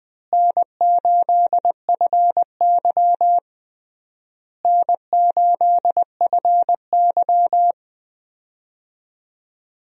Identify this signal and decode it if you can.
Morse code, heard as a keyed tone, about 20 wpm, 700 Hz. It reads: N8FY N8FY
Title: USA Call Signs - 4 Characters - Repeated 3x - 20wpm